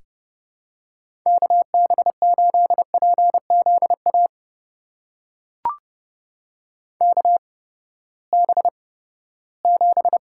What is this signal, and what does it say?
Morse code, heard as a keyed tone, about 32 wpm, spaced as for 17 wpm, 700 Hz. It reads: KB8PZA KB7
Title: USA Call Signs - 6 Characters - 8x Character Spacing - 30wpm